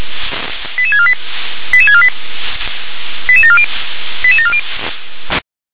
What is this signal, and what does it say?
Five-tone selective call (ZVEI), heard as quick runs of five tones, tones of about 70 ms. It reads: 88648 88648 88740 88740
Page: ZVEI